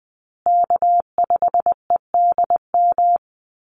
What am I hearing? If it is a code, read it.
Morse code, heard as a keyed tone, about 20 wpm, 700 Hz. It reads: K5EDM
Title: Call Signs Repeated 3x - 20wpm